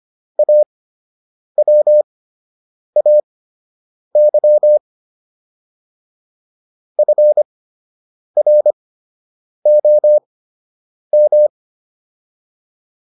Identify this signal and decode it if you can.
Morse code, heard as a keyed tone, about 25 wpm, spaced as for 8 wpm, 600 Hz. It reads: AWAY FROM